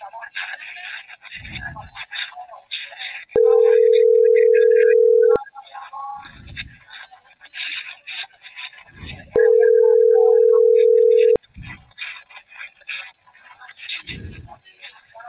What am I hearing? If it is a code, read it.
Ringback tone (North America), 440 + 480 Hz.